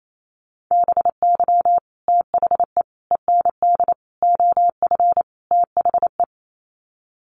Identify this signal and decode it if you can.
Morse code, heard as a keyed tone, about 28 wpm, 700 Hz. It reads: BY THE END OF THE
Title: Sets of 5 Words 28wpm